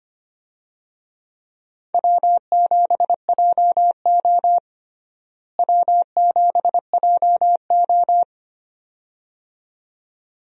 Morse code, heard as a keyed tone, about 25 wpm, 700 Hz. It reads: W7JO W7JO
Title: USA Call Signs - 4 Characters - Repeated 2x - 25wpm